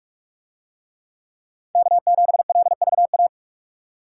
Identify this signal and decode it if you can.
Morse code, heard as a keyed tone, about 46 wpm, 700 Hz. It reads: K7LVA